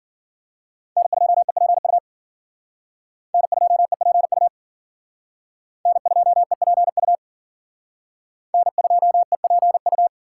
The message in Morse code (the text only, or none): N2EPU N2EPU N2EPU N2EPU